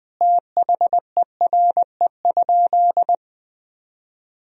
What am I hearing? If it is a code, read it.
Morse code, heard as a keyed tone, about 20 wpm, 700 Hz. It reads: THERE?
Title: Sentences from Top 200 Words and Abbreviations 20wpm